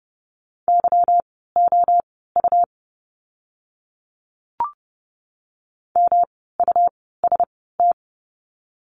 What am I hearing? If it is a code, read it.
Morse code, heard as a keyed tone, about 30 wpm, spaced as for 17 wpm, 700 Hz. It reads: YOU MUST
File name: course / 30:10wpm / 237 Lesson 38f - Letter q - Words 30wpm